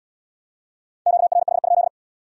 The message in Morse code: QRH?

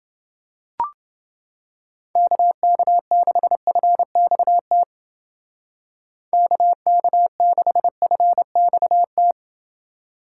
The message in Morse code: KK6FXT KK6FXT

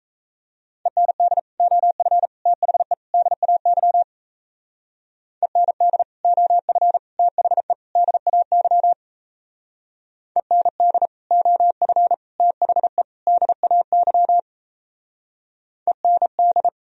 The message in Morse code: END OF THE DAY END OF THE DAY END OF THE DAY END

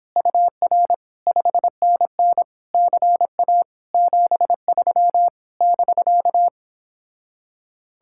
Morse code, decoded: UR 5NN CA 73 <BK>

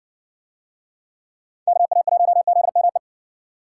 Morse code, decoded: KA1ZME

